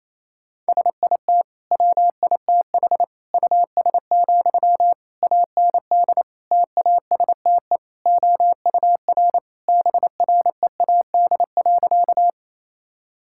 Morse code, heard as a keyed tone, about 28 wpm, 700 Hz. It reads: SIT WITH US, AND TASTE OUR BREAD.
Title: Sentences from Top 1600 Words 28wpm